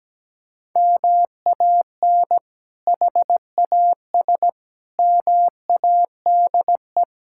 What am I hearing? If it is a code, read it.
Morse code, heard as a keyed tone, about 17 wpm, 700 Hz. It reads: MAN HAS MADE
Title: Sentences from Top 1000 Words - Review Part 6 - 17wpm